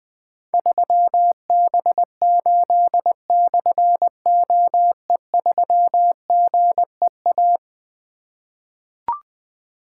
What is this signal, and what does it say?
Morse code, heard as a keyed tone, about 21 wpm, 700 Hz. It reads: 3B8/OE3GEA
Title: Difficult Call Signs 20wpm